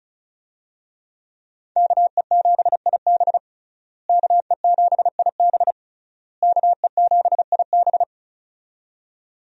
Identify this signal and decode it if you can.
Morse code, heard as a keyed tone, about 36 wpm, 700 Hz. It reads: KE7IB KE7IB KE7IB
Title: USA Call Signs - 5 Characters - Repeated 3x - 35wpm